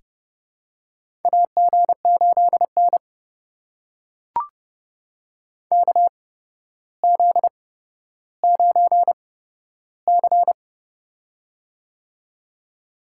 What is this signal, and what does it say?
Morse code, heard as a keyed tone, about 30 wpm, spaced as for 13 wpm, 700 Hz. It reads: AG8N KZ9C
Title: USA Call Signs - 4 Characters - 8x Character Spacing - 30wpm